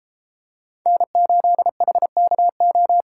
Morse code, heard as a keyed tone, about 33 wpm, 700 Hz. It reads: N8HKO